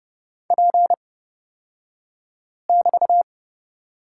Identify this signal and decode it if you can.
Morse code, heard as a keyed tone, about 31 wpm, spaced as for 5 wpm, 700 Hz. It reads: P<BT>
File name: Lesson 36d - BT - Two Characters 30wpm